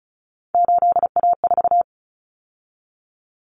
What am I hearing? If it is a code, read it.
Morse code, heard as a keyed tone, about 34 wpm, 700 Hz. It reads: 8A4
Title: Common Cut Numbers - 3-Digit 35wpm